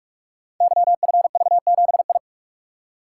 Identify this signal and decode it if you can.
Morse code, heard as a keyed tone, about 44 wpm, 700 Hz. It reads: YFV7I
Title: Code Groups - 5 Letter:Numbers - 45wpm